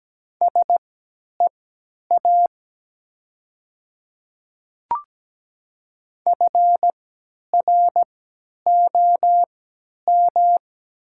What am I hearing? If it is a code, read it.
Morse code, heard as a keyed tone, about 17 wpm, spaced as for 10 wpm, 700 Hz. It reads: SEA FROM
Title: 3x Character Spacing - Top 500 Words - 17wpm